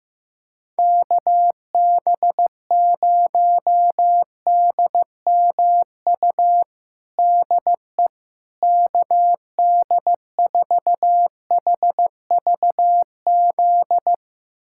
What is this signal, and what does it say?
Morse code, heard as a keyed tone, about 15 wpm, 700 Hz. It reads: KB0DMU DE KD4HVZ
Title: USA Callsign DE Callsign 15wpm